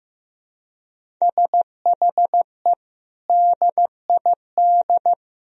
Morse code, heard as a keyed tone, about 15 wpm, 700 Hz. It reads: SHE DID
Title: Sentences from Top 1000 Words - Review Part 1 - 15wpm